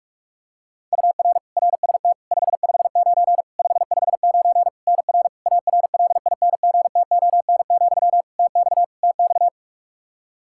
Morse code, heard as a keyed tone, about 46 wpm, 700 Hz. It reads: UR RST 559 559 NR ARLINGTON, TX TX
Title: Practice QSO - Exchange 1 - Beginning - 45wpm